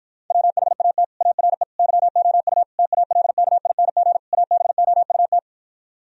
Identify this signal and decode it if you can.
Morse code, heard as a keyed tone, about 60 wpm, 700 Hz. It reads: WHAT ARE YOU TALKING ABOUT